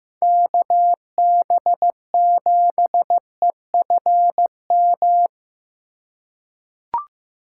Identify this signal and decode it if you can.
Morse code, heard as a keyed tone, about 15 wpm, 700 Hz. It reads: KB7EFM
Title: USA Call Signs - 6 Characters - 4x Character Spacing - 15wpm